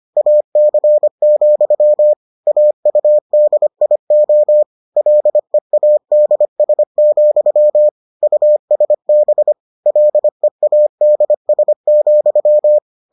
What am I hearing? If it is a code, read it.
Morse code, heard as a keyed tone, about 25 wpm, 600 Hz. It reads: AC, AUDIO LEADS, USB LEADS,